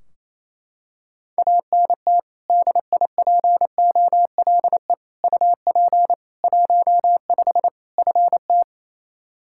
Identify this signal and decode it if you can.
Morse code, heard as a keyed tone, about 28 wpm, 700 Hz. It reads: ANT DIPOLE UP 15 FT
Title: Practice QSO - Exchange 2 - Info - 28wpm